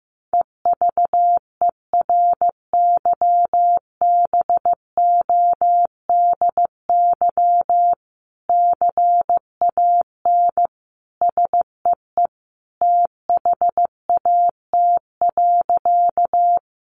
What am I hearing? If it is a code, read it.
Morse code, heard as a keyed tone, about 15 wpm, 700 Hz. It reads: EVERYBODY CAN SEE THAT.
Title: Sentences from Top 1400 Words 15wpm